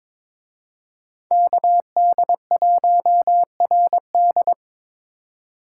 Morse code, heard as a keyed tone, about 25 wpm, 700 Hz. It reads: KD1RD
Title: Speed Racing - USA 2x2 Call Signs 22wpm